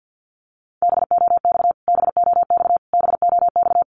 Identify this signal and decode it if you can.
Morse code, heard as a keyed tone, about 50 wpm, 700 Hz. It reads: BOX BOX BOX
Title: Top 500 Words Repeated 50wpm